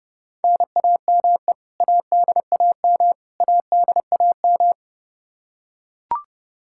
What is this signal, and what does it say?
Morse code, heard as a keyed tone, about 30 wpm, 700 Hz. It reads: NAME ADAM ADAM
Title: K1USN SST QSO Practice 30wpm